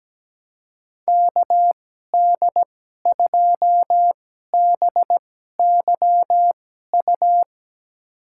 Morse code, heard as a keyed tone, about 17 wpm, spaced as for 12 wpm, 700 Hz. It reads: KD2BYU
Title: USA Call Signs - 6 Characters - 2x Character Spacing - 17wpm